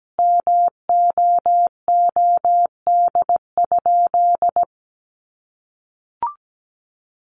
Morse code, heard as a keyed tone, about 17 wpm, 700 Hz. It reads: MOOD?
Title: Sentences from Top 1100 Words 17wpm